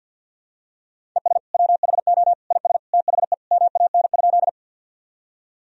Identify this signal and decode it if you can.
Morse code, heard as a keyed tone, about 49 wpm, 700 Hz. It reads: ES WHO IS THE MAN?
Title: Sentences from Top 200 Words and Abbreviations 50wpm